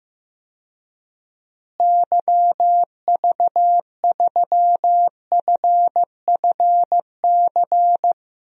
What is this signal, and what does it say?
Morse code, heard as a keyed tone, about 15 wpm, 700 Hz. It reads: YV3FFC
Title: DXCC-100 Call Signs - 6 Characters - Repeated 3x - 15wpm